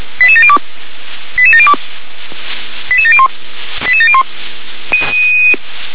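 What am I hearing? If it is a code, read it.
Five-tone selective call (ZVEI), heard as quick runs of five tones, tones of about 70 ms. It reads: 88702 88702 88701 88701
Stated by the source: ZVEI five-tone selective call 88702 received on SDR.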